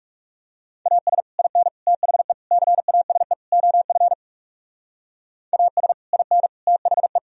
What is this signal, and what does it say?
Morse code, heard as a keyed tone, about 38 wpm, 700 Hz. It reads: AS IN THE CASE OF AS IN THE
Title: Speed-Racing - Sets of 5 Words 30wpm